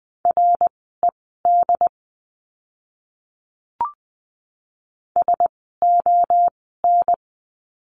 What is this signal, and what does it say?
Morse code, heard as a keyed tone, about 20 wpm, spaced as for 15 wpm, 700 Hz. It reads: RED SON
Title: 094 Lesson 16f - Letter c - Words 20wpm